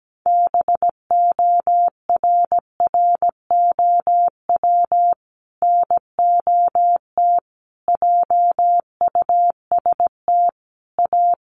Morse code, heard as a keyed tone, about 17 wpm, 700 Hz. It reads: BORROW NOT JUST A
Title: Sentences from Top 1500 Words 17wpm